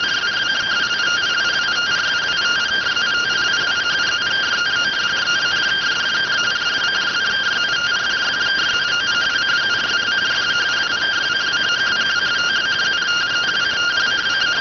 Two-tone teleprinter signal, about 170 Hz shift.